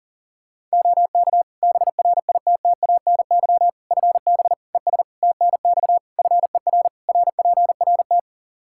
Morse code, heard as a keyed tone, about 37 wpm, 700 Hz. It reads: OK BRITTANY FB ES TNX FER RPRT
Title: Practice QSO - Exchange 2 - Info - 40wpm